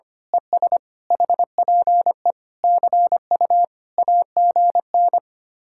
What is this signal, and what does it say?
Morse code, heard as a keyed tone, about 24 wpm, 700 Hz. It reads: ES HPE CU AGN
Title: Speed-Racing Common QSO Phrases Encore - 25wpm